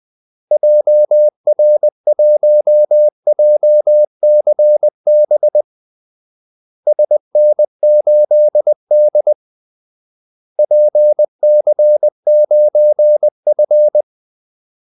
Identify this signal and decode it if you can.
Morse code, heard as a keyed tone, about 20 wpm, 600 Hz. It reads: JR1JCB SN8D PC9F
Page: Kaldesignaler 20 wpm